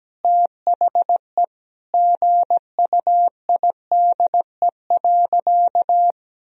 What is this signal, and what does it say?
Morse code, heard as a keyed tone, about 17 wpm, 700 Hz. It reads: THE GUIDE.
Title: Sentences from Top 1000 Words - Review Part 3 - 17wpm